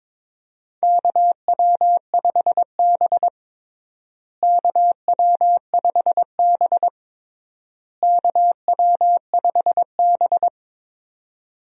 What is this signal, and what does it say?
Morse code, heard as a keyed tone, about 22 wpm, 700 Hz. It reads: KW5B KW5B KW5B
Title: USA Call Signs - 4 Characters - Repeated 3x - 22wpm